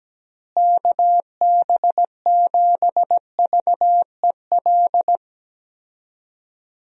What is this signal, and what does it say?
Morse code, heard as a keyed tone, about 17 wpm, 700 Hz. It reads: KB7VEL
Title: USA Call Signs - 6 Characters - Repeated 3x - 17wpm